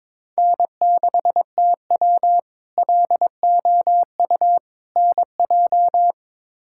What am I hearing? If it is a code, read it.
Morse code, heard as a keyed tone, about 22 wpm, 700 Hz. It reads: N6TW LOU NJ
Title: K1USN SST QSO Practice - 22wpm